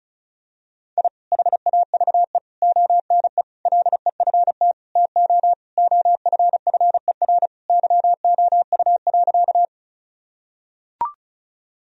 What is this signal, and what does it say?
Morse code, heard as a keyed tone, about 36 wpm, 700 Hz. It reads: I HAVE ONE LEFT TO OFFER YOU.
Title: Sentences from Top 1000 Words - Review Part 4 - 35wpm